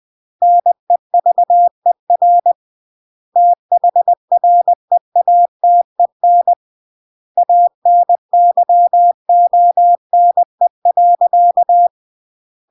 Morse code, heard as a keyed tone, about 20 wpm, 700 Hz. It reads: NEVER THREATEN ANYONE.
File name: CW / jodorowsky / 82 commandments 20-20